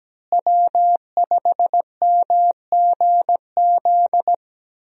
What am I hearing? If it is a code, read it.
Morse code, heard as a keyed tone, about 17 wpm, 700 Hz. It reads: W5MGZ
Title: USA Call Signs - 5 Characters - Repeated 2x - 17wpm